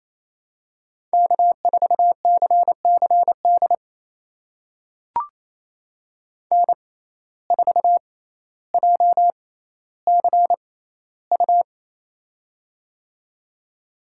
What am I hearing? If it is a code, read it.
Morse code, heard as a keyed tone, about 28 wpm, spaced as for 14 wpm, 700 Hz. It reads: K4CCD N4JCU
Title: USA Call Signs - 5 Characters - 6x Character Spacing - 28wpm